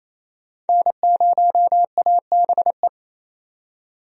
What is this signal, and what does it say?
Morse code, heard as a keyed tone, about 28 wpm, 700 Hz. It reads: N0ABE
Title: USA Call Signs - 5 Characters - 28wpm